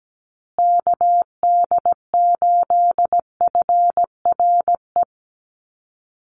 Morse code, heard as a keyed tone, about 17 wpm, 700 Hz. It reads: KD8FRE
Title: USA Call Signs Repeated 2x 17wpm